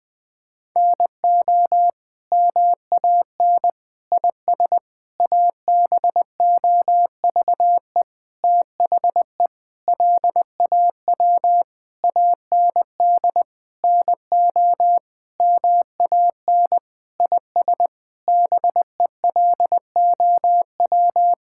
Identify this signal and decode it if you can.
Morse code, heard as a keyed tone, about 20 wpm, 700 Hz. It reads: NO MAN IS ABOVE THE LAW AND NO MAN IS BELOW
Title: Sentences from Top 1000 Words - Review Part 5 - 20wpm